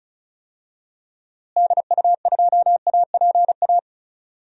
Morse code, heard as a keyed tone, about 36 wpm, 700 Hz. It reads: DU2APA
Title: DXCC-100 Call Signs - 6 Characters - Repeated 3x - 35wpm